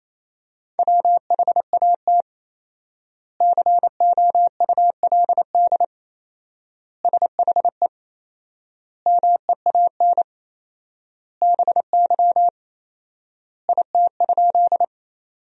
Morse code, heard as a keyed tone, about 28 wpm, 700 Hz. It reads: WHAT COULD SHE MEAN BY IT?
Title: 4x Word-Spacing - Sentences from Top 200 Words - 28wpm